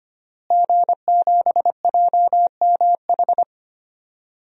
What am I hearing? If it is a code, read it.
Morse code, heard as a keyed tone, about 25 wpm, 700 Hz. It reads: G7JMH
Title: DXCC-100 Call Signs - 5 Characters - Repeated 3x - 25wpm